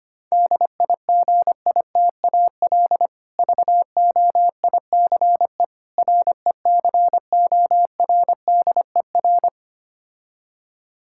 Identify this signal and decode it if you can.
Morse code, heard as a keyed tone, about 26 wpm, 700 Hz. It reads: DIGITAL VOICE RECORDER
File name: practice / 25wpm / Difficult Words - Speed-Racing with Hints - 25wpm